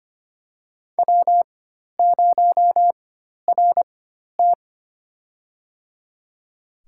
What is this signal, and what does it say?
Morse code, heard as a keyed tone, about 25 wpm, spaced as for 12 wpm, 700 Hz. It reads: W0RT
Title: USA Call Signs - 4x Character Spacing - 25wpm